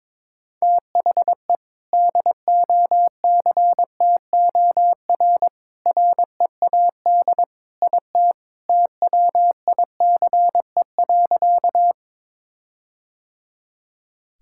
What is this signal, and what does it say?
Morse code, heard as a keyed tone, about 21 wpm, 700 Hz. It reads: THE DOCTOR READ IT TWICE.
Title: Sentences from Top 1600 Words 22wpm